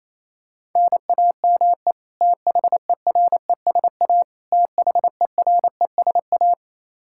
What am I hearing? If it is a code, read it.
Morse code, heard as a keyed tone, about 28 wpm, 700 Hz. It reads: NAME THERESA THERESA
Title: Practice QSO - Exchange 1 - Beginning - 28wpm